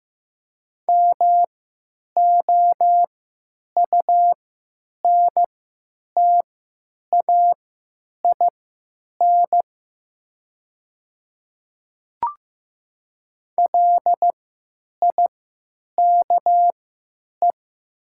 Morse code, heard as a keyed tone, about 15 wpm, spaced as for 9 wpm, 700 Hz. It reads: MOUNTAIN LIKE
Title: 3x Character Spacing - Top 300 Words - 15wpm